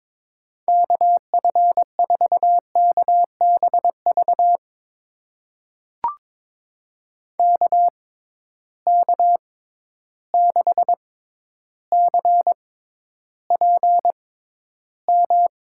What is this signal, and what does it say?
Morse code, heard as a keyed tone, about 22 wpm, spaced as for 11 wpm, 700 Hz. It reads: KF4KBV KK6CPM
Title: USA Call Signs - 6 Characters - 6x Character Spacing - 22wpm